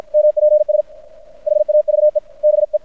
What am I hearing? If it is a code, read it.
Morse code, heard as a keyed tone, about 53 wpm, 600 Hz. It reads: MPS HAVE BE
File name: archived / CRSnet